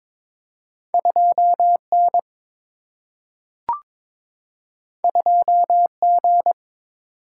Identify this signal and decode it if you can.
Morse code, heard as a keyed tone, about 22 wpm, 700 Hz. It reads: 2N 2G
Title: Single-Letter Single-Number 22wpm